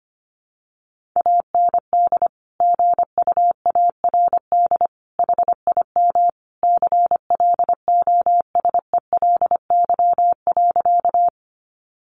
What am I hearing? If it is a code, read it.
Morse code, heard as a keyed tone, about 25 wpm, 700 Hz. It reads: AND GUARD HIM CLOSELY.
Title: Sentences from Top 1400 Words 25wpm